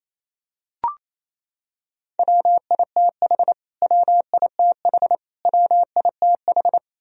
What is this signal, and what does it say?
Morse code, heard as a keyed tone, about 28 wpm, 700 Hz. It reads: WITH WITH WITH
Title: Top 100 Words Repeated 28wpm